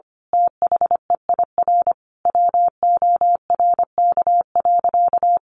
Morse code, heard as a keyed tone, about 25 wpm, 700 Hz. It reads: THEIR WORK.
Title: Sentences from Top 1100 Words 25wpm